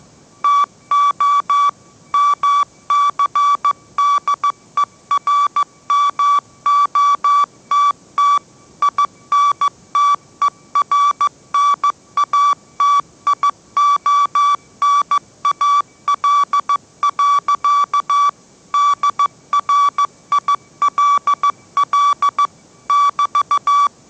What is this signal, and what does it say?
Morse code, heard as a keyed tone, about 15 wpm, 1150 Hz. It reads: TO MCDERMOTT INTERNATIONAL. DRILL <BT>
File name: mrx / mrx065 16wpm